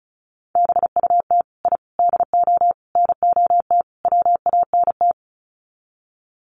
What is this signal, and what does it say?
Morse code, heard as a keyed tone, about 34 wpm, 700 Hz. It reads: BUT I DO NOT WANT
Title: Sets of 5 Words 35wpm